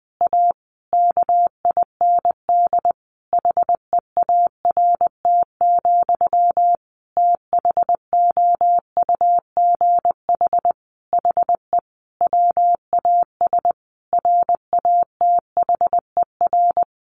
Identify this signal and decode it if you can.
Morse code, heard as a keyed tone, about 20 wpm, 700 Hz. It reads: A KIND HEART, THOUGH HE WAS RATHER
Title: Sentences from Top 1400 Words 20wpm